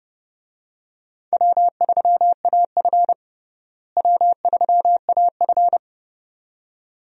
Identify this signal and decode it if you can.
Morse code, heard as a keyed tone, about 30 wpm, 700 Hz. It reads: W3AF W3AF
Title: USA Call Signs - 4 Characters - Repeated 2x - 30wpm